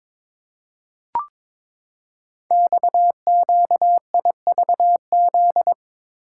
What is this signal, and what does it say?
Morse code, heard as a keyed tone, about 22 wpm, 700 Hz. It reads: XQIVZ
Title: Code Groups - 5 Letters - 22wpm